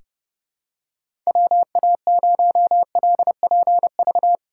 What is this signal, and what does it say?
Morse code, heard as a keyed tone, about 29 wpm, 700 Hz. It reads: WA0LPV
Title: K1USN SST QSO Practice - 30wpm